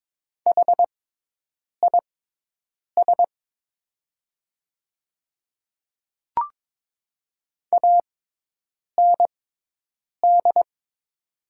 Morse code, heard as a keyed tone, about 22 wpm, spaced as for 8 wpm, 700 Hz. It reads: HIS AND
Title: 6x Character Spacing - Top 100 Words - 22wpm